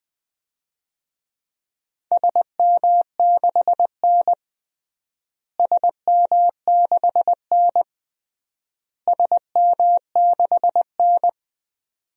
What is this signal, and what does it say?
Morse code, heard as a keyed tone, about 20 wpm, 700 Hz. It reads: SM6N SM6N SM6N
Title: DXCC-100 Call Signs - 4 Characters - Repeated 3x - 20wpm